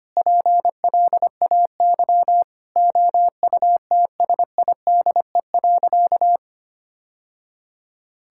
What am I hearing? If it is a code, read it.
Morse code, heard as a keyed tone, about 25 wpm, 700 Hz. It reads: PLAY OUTSIDE.
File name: practice / 25wpm / Sentences with 3-Word Phrases - 25wpm